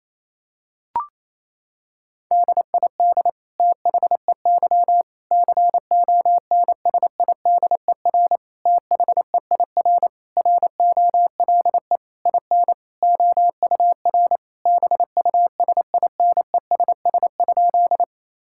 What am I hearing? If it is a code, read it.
Morse code, heard as a keyed tone, about 28 wpm, 700 Hz. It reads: DID THEY CONSIDER THEIR ROLE IN OUR BUSINESS?
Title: Sentences from Top 1000 Words - Review Part 6 - 28wpm